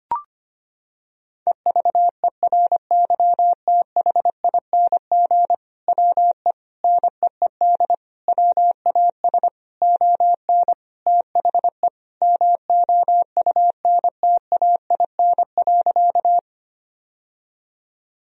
Morse code, heard as a keyed tone, about 25 wpm, 700 Hz. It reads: EVERYTHING WE NEED WAS ON THE MOUNTAIN.
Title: Sentences from Top 1000 Words - Review Part 1 - 25wpm